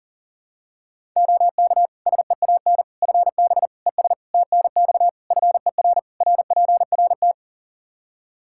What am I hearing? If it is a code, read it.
Morse code, heard as a keyed tone, about 40 wpm, 700 Hz. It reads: OK SEAN FB ES TNX FER RPRT
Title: Practice QSO - Exchange 2 - Info - 40wpm